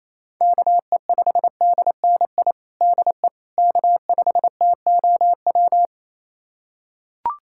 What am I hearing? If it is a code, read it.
Morse code, heard as a keyed tone, about 27 wpm, 700 Hz. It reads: KE5DNI DE K5TOW
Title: USA Callsign DE Callsign 28wpm